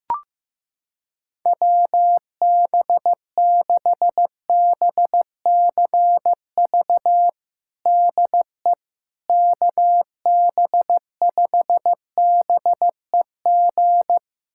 Morse code, heard as a keyed tone, about 15 wpm, 700 Hz. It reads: WB6BCV DE KB5BEG
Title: USA Callsign DE Callsign - 6 Characters - 15wpm